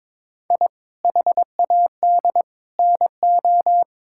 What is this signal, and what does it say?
Morse code, heard as a keyed tone, about 22 wpm, 700 Hz. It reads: I HAD NO
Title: Sentences from Top 1000 Words - Review Part 6 - 22wpm